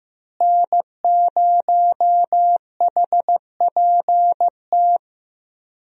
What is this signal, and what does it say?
Morse code, heard as a keyed tone, about 15 wpm, 700 Hz. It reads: N0HPT
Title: USA Call Signs - 5 Characters - Repeated 3x - 15wpm